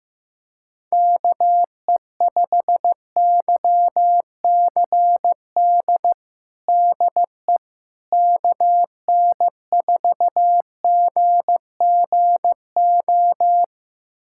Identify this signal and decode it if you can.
Morse code, heard as a keyed tone, about 15 wpm, 700 Hz. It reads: KE5YCD DE KN4GGO
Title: USA Callsign DE Callsign - 6 Characters - 15wpm